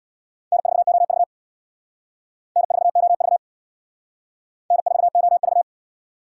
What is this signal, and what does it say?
Morse code, heard as a keyed tone, about 65 wpm, 700 Hz. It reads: N4OV N4OV N4OV